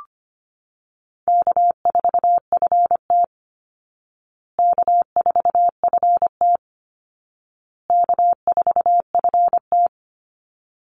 Morse code, heard as a keyed tone, about 25 wpm, 700 Hz. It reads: K4FT K4FT K4FT
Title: Easy Call Signs Repeated 3x - 25wpm